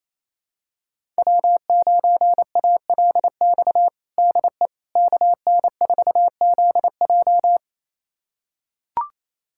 Morse code, heard as a keyed tone, about 28 wpm, 700 Hz. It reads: W9ALX DE KN4ZJ
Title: USA Callsign DE Callsign - 5 Characters - 28wpm